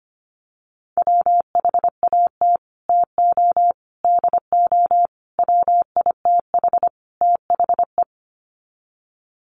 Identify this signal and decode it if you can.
Morse code, heard as a keyed tone, about 25 wpm, 700 Hz. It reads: WHAT TO DO WITH THE
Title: Sets of 5 Words Encore2 25wpm